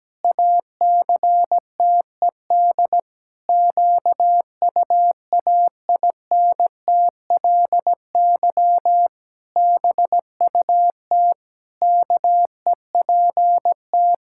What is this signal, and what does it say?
Morse code, heard as a keyed tone, about 17 wpm, 700 Hz. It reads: ACTED QUAINTLY BUT KEPT
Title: Pangram Challenge v2 - 17wpm